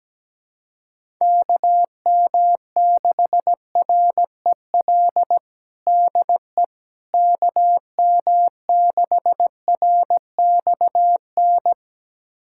Morse code, heard as a keyed tone, about 17 wpm, 700 Hz. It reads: KM6REL DE KM6RXN
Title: USA Callsign DE Callsign - 6 Characters - 17wpm